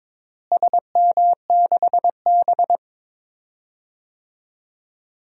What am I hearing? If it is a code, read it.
Morse code, heard as a keyed tone, about 22 wpm, 700 Hz. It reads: SM6B
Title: DXCC-100 Call Signs - 4 Characters - Repeated 3x - 22wpm